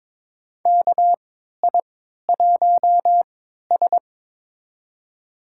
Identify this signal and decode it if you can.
Morse code, heard as a keyed tone, about 22 wpm, spaced as for 13 wpm, 700 Hz. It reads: KI1S
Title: USA Call Signs - 4 Characters - 3x Character Spacing - 22wpm